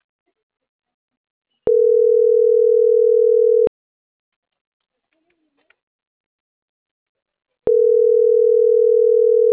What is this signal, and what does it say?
Ringback tone (North America), 440 + 480 Hz.